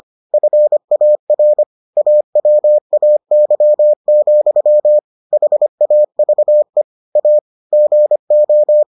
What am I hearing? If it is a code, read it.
Morse code, heard as a keyed tone, about 25 wpm, 600 Hz. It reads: FAR AWAY, HAVE A GO